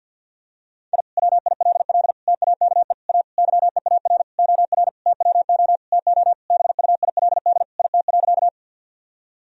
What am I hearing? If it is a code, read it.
Morse code, heard as a keyed tone, about 51 wpm, 700 Hz. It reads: I WILL TAKE A YEAR OR TWO TO BUILD IT.
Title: Sentences from Top 1000 Words - Review Part 1 - 50wpm